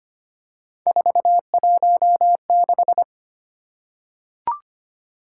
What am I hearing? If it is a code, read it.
Morse code, heard as a keyed tone, about 25 wpm, 700 Hz. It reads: 416
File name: Three Letter-Number Combinations - 25wpm